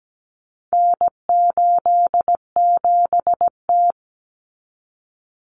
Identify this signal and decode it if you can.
Morse code, heard as a keyed tone, about 17 wpm, 700 Hz. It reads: N87T
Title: Common Cut Numbers - 4-Digit 17wpm